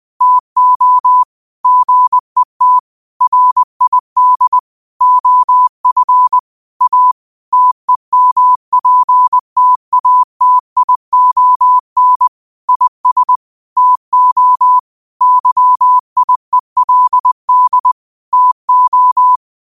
Morse code, heard as a keyed tone, about 20 wpm, 1000 Hz. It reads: TO GET RID OF A TEMPTATION IS TO YIELD TO